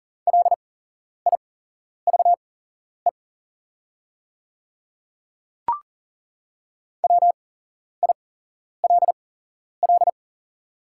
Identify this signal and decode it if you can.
Morse code, heard as a keyed tone, about 40 wpm, spaced as for 11 wpm, 690 Hz. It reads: LIVE WILL
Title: 8x Character Spacing - Top 500 Words - 40wpm